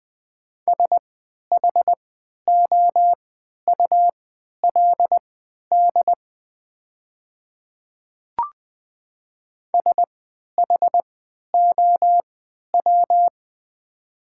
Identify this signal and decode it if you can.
Morse code, heard as a keyed tone, about 20 wpm, spaced as for 11 wpm, 700 Hz. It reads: SHOULD SHOW
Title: 3x Character Spacing - Top 300 Words - 20wpm